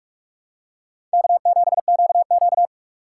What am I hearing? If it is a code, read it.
Morse code, heard as a keyed tone, about 46 wpm, 700 Hz. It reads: K7QQ
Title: USA Call Signs - 4 Characters - 2x Character Spacing - 45wpm